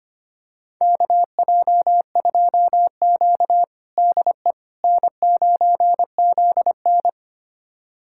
Morse code, heard as a keyed tone, about 25 wpm, 700 Hz. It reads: KJ2Q DE N9ZN
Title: USA Callsign DE Callsign - 4 Characters - 25wpm